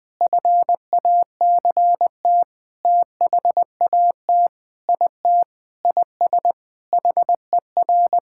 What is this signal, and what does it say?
Morse code, heard as a keyed tone, about 20 wpm, 700 Hz. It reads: FACT THAT IT IS HER
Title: Sentences with 6-Word Phrases - 20wpm